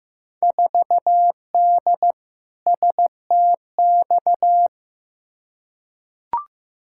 Morse code, heard as a keyed tone, about 15 wpm, 700 Hz. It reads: 4D STX
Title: Field Day Exchanges Encore - 15wpm